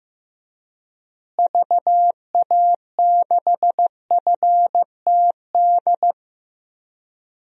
Morse code, heard as a keyed tone, about 15 wpm, 700 Hz. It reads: VA6FTD
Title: DXCC-100 Call Signs - 6 Characters - Repeated 3x - 15wpm